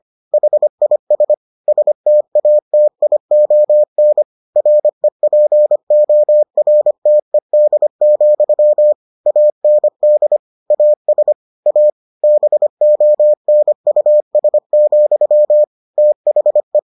Morse code, heard as a keyed tone, about 25 wpm, 600 Hz. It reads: HIS STATION REPORTED, AND AS A BONUS, THE